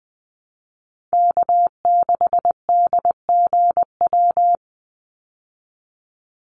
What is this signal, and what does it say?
Morse code, heard as a keyed tone, about 20 wpm, 700 Hz. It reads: K6DGW
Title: Call Signs Repeated 3x - 20wpm